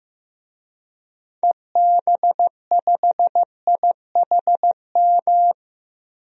Morse code, heard as a keyed tone, about 15 wpm, 700 Hz. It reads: EB5IHM
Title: DXCC-100 Call Signs - 6 Characters - Repeated 3x - 15wpm